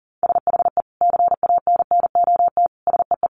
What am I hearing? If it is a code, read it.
Morse code, heard as a keyed tone, about 40 wpm, 700 Hz. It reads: SHE CANNOT SEE
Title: Additional Easy Phrases 40wpm